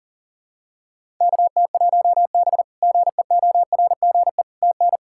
Morse code, heard as a keyed tone, about 40 wpm, 700 Hz. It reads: KT1B GEORGE TN
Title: K1USN SST QSO Practice - 40wpm